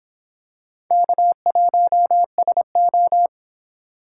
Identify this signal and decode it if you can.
Morse code, heard as a keyed tone, about 26 wpm, 700 Hz. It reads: K1SO